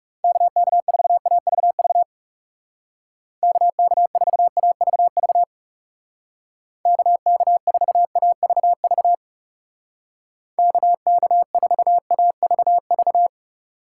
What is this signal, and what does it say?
Morse code, heard as a keyed tone, about 37 wpm, 700 Hz. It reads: KK4AVV KK4AVV KK4AVV KK4AVV
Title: Speed Racing - USA 2x3 Call Signs 30wpm